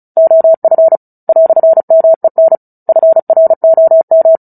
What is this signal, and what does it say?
Morse code, heard as a keyed tone, about 34 wpm, 650 Hz. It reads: OF "MEN FROM
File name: Book Two / 09-Chapter 9 - 35 WPM